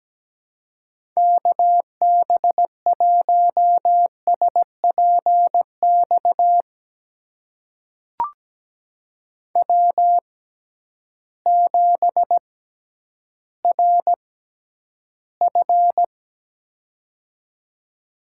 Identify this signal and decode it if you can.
Morse code, heard as a keyed tone, about 17 wpm, spaced as for 10 wpm, 700 Hz. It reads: KB1SPX W7RF
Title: USA Call Signs - 6x Character Spacing - 17wpm